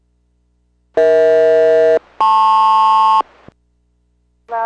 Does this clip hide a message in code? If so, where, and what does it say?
Yes: aircraft SELCAL from 1.0 to 3.2 s, heard as two pairs of tones. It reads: EH-LP